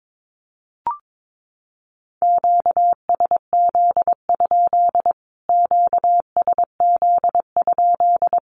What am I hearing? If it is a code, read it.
Morse code, heard as a keyed tone, about 22 wpm, 700 Hz. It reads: QSZ? QSZ?
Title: Q-Codes Repeated 3x 22wpm